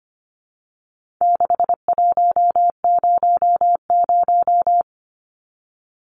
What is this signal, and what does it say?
Morse code, heard as a keyed tone, about 25 wpm, 700 Hz. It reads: 6100
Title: Numbers - 4-Digit 25wpm